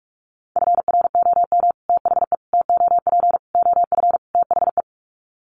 Morse code, heard as a keyed tone, about 44 wpm, 700 Hz. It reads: FROM THE TOP OF THE
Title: Sets of 5 Words Encore 45wpm